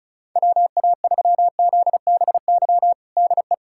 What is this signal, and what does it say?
Morse code, heard as a keyed tone, about 34 wpm, 700 Hz. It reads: WA3ZBY DE
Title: USA Callsign DE Callsign 35wpm